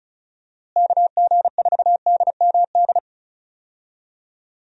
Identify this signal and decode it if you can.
Morse code, heard as a keyed tone, about 34 wpm, 700 Hz. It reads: KG4DMD
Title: USA Call Signs - 6 Characters - Repeated 3x - 35wpm